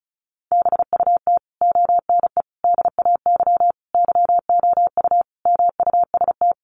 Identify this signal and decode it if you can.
Morse code, heard as a keyed tone, about 34 wpm, 700 Hz. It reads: BUT ONE DAY YOU MUST
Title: Sentences from Top 400 Words 35wpm